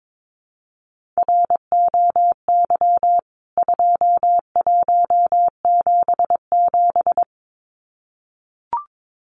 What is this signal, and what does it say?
Morse code, heard as a keyed tone, about 22 wpm, 700 Hz. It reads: ROY 2177
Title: CWT Exchange 22wpm